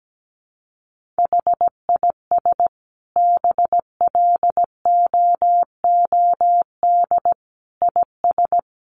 Morse code, heard as a keyed tone, about 17 wpm, 700 Hz. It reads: HIS BLOOD IS
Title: Sentences from Top 1400 Words 17wpm